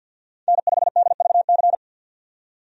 Morse code, heard as a keyed tone, about 52 wpm, 700 Hz. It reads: N5DVC
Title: USA Call Signs - 5 Characters - 50wpm